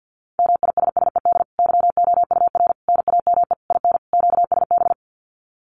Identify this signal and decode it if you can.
Morse code, heard as a keyed tone, about 48 wpm, 700 Hz. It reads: MISSED YOUR NAME IN QSB
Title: Common QSO Phrases 50wpm